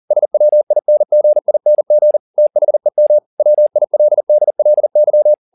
Morse code, heard as a keyed tone, about 40 wpm, 600 Hz. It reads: SWINGING THEM WILDLY